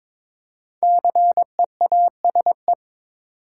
Morse code, heard as a keyed tone, about 22 wpm, 700 Hz. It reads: CEASE
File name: 5 Letter Words - First 3 Letters Given - 22wpm